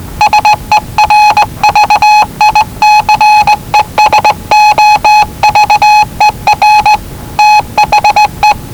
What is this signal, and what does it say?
Morse code, heard as a keyed tone, about 19 wpm, 840 Hz. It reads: SERVICES OVER THE
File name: WellingtonRadioZlwCloseDownOnHfAndMf / 01-Zlw 4277khz 417.5 qrt until 0925z